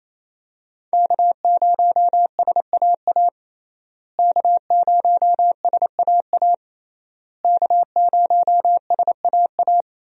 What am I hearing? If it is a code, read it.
Morse code, heard as a keyed tone, about 27 wpm, 700 Hz. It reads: K0SAA K0SAA K0SAA
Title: USA Call Signs - 5 Characters - Repeated 3x - 28wpm